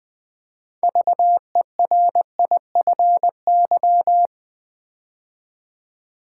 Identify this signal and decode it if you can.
Morse code, heard as a keyed tone, about 20 wpm, 700 Hz. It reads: VERIFY
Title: 6 Letter Words - First 3 Letters Given - 20wpm